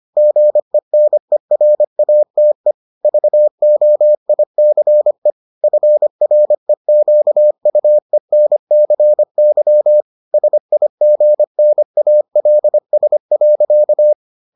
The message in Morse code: GENERATE VOICE FREQUENCY SIGNALS.

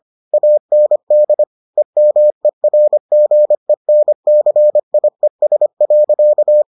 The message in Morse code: AND EMERGENCIES.

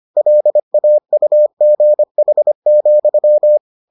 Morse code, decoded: LAUGH,

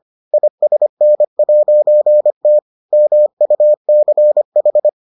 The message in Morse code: ISN'T MUCH

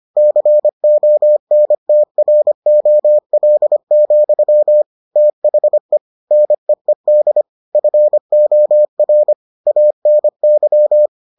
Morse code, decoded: CONTROL, THE NEED FOR ANY

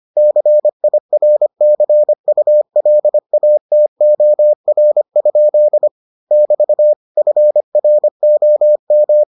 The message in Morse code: CIRCULATOR? <BT> FROM

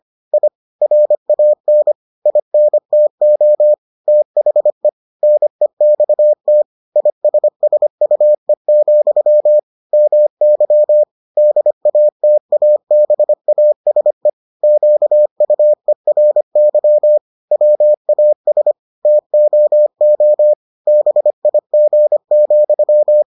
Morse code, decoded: I RAN INTO THE NEXT ISSUE, MY DATABASE QUERY WAS TOO BIG,

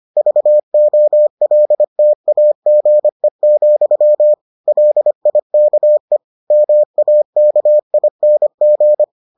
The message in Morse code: VOLTAGE, LIKE MAKING

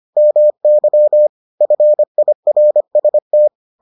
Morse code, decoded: MY FIRST

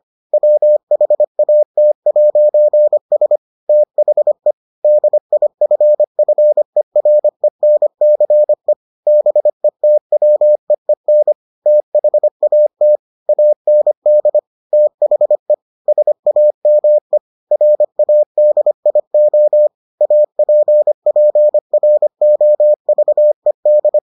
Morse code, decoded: WHAT'S THE DIFFERENCE BETWEEN THAT AND THE SAME RADIO APPROVED

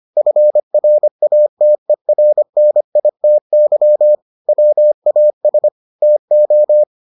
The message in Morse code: FRATERNITY WAS TO

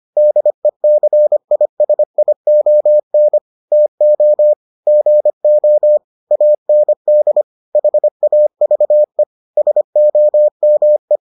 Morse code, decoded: DECISION TO GO AND HAVE SOME

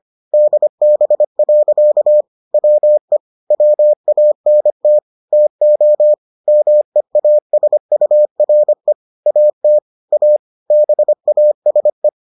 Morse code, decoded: DB. WE WANT TO MEASURE AT A BASE

